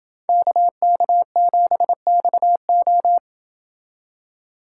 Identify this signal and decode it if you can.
Morse code, heard as a keyed tone, about 27 wpm, 700 Hz. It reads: KK7XO